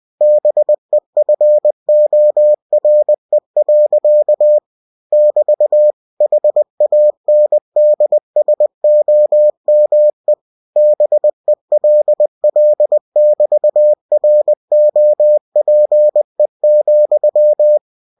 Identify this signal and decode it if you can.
Morse code, heard as a keyed tone, about 20 wpm, 600 Hz. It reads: BEFORE. <BT> HANDSOME BELL<BT>ROPE,
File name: cw-books / winnie-the-pooh / winnie-20wpm-page-0031